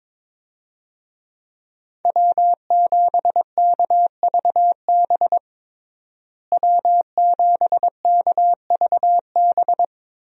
Morse code, heard as a keyed tone, about 22 wpm, 700 Hz. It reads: W7KVB W7KVB
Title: USA Call Signs - 5 Characters - Repeated 3x - 22wpm